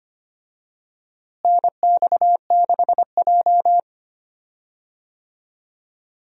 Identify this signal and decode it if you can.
Morse code, heard as a keyed tone, about 25 wpm, 700 Hz. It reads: NX6J
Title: USA Call Signs - 4 Characters - Repeated 3x - 25wpm